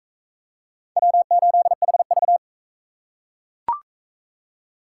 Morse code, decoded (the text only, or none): W8HV